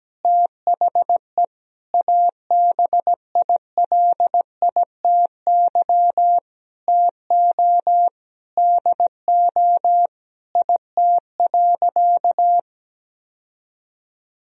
Morse code, heard as a keyed tone, about 17 wpm, 700 Hz. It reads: THE ABILITY TO DO IT.
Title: Sentences from Top 1000 Words - Review Part 3 - 17wpm